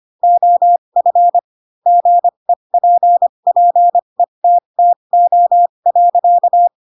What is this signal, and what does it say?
Morse code, heard as a keyed tone, about 25 wpm, spaced as for 22 wpm, 700 Hz. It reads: OF GEPPETTO.
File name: Pinocchio / Pinocchio ch6 18 25 ik1hge